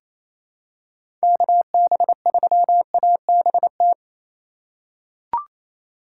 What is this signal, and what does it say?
Morse code, heard as a keyed tone, about 28 wpm, 700 Hz. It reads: KB3ABT
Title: USA Call Signs - 6 Characters - 4x Character Spacing - 28wpm